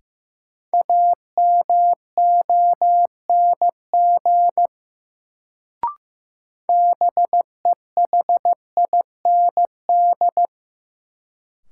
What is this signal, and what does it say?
Morse code, heard as a keyed tone, about 15 wpm, 700 Hz. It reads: AMONG BEHIND
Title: Sentences from Top 500 Words - New Word Drill - 15wpm